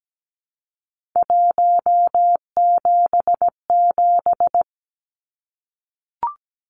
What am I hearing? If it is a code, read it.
Morse code, heard as a keyed tone, about 17 wpm, 700 Hz. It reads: 177
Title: Numbers - 3-Digit 17wpm